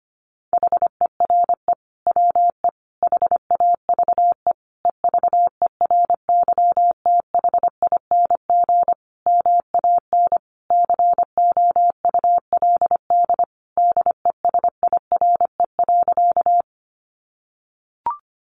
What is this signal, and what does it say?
Morse code, heard as a keyed tone, about 25 wpm, 700 Hz. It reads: HERE WE HAVE EVERYTHING MAN COULD DESIRE.
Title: Sentences from Top 1100 Words 25wpm